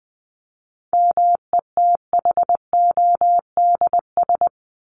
Morse code, heard as a keyed tone, about 20 wpm, 700 Hz. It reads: METHODS
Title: ES:S Suffix - Top 100 Words with Base Words 20wpm